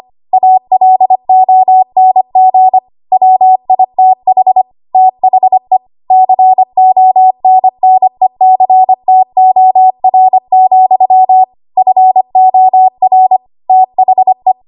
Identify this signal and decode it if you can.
Morse code, heard as a keyed tone, about 25 wpm, 750 Hz. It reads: ALONG WITH THE CONNECTOR, FOR THE
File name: Archive / 25 WPM / 25WPM